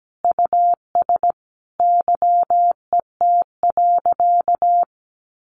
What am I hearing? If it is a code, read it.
Morse code, heard as a keyed tone, about 17 wpm, 700 Hz. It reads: US YET.
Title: A Sound Of Thunder 17wpm